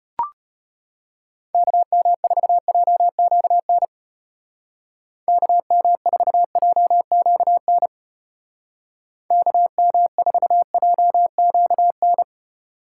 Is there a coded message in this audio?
Yes: Morse code from 1.5 to 12.2 s, heard as a keyed tone, about 31 wpm, 700 Hz. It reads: KM4JQN KM4JQN KM4JQN